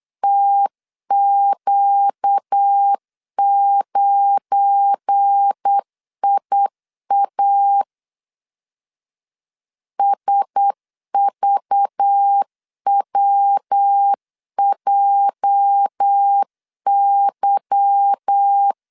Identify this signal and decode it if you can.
Morse code, heard as a keyed tone, about 8 wpm, 800 Hz. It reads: TQ9IA SVWJY
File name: LEZIONI / 8 YJ45 / 8 YJ 8wpm 3313